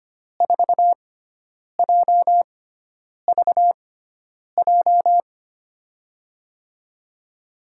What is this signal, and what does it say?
Morse code, heard as a keyed tone, about 25 wpm, spaced as for 9 wpm, 700 Hz. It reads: 4JVJ
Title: USA Call Signs - 6x Character Spacing - 25wpm